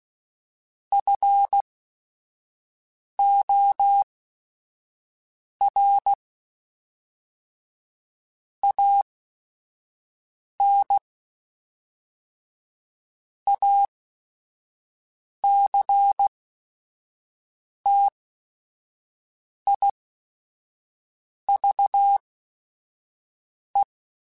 Morse code, heard as a keyed tone, about 16 wpm, spaced as for 5 wpm, 800 Hz. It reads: FOR AN ACTIVE
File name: Technical Subjects / Technical Subjects-mp3s / AND Gate @5